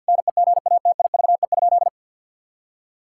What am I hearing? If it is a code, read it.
Morse code, heard as a keyed tone, about 50 wpm, 700 Hz. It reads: NEGATIVE?
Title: Sentences from Top 1000 Words - Review Part 6 - 50wpm